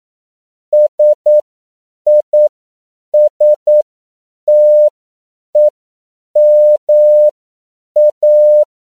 Morse code, heard as a keyed tone, about 9 wpm, spaced as for 7 wpm, 600 Hz. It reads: SISTEMA
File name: CW / 10PPM